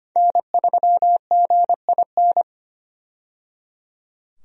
Morse code, heard as a keyed tone, about 25 wpm, 700 Hz. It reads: N3GIN
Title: USA Call Signs - 5 Characters - 25wpm